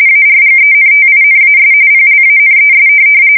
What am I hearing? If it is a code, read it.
RTTY, about 45.45 baud. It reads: ARGENTINAS PLANES
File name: fsk170 / USS-New-Jersey